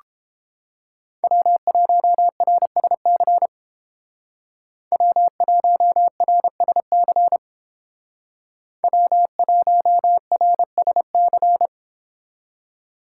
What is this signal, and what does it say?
Morse code, heard as a keyed tone, about 29 wpm, 700 Hz. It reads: W1RSC W1RSC W1RSC